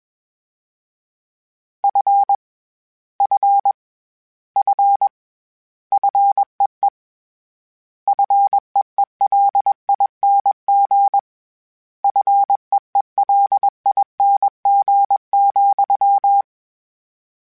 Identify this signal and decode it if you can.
Morse code, heard as a keyed tone, about 21 wpm, 800 Hz. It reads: F F F FEE FEELING FEELING,
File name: Short Runs / Short Runs-mp3s / SinglePrefixes @20